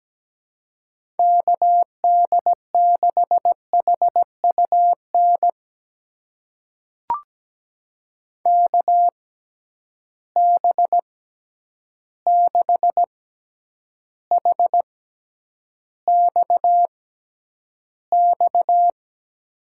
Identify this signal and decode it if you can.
Morse code, heard as a keyed tone, about 17 wpm, spaced as for 9 wpm, 700 Hz. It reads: KD6HUN KB6HXX
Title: USA Call Signs - 6 Characters - 6x Character Spacing - 17wpm